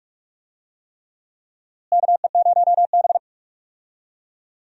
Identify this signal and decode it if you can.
Morse code, heard as a keyed tone, about 46 wpm, 700 Hz. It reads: KE0B
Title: USA Call Signs - 4 Characters - Repeated 3x - 45wpm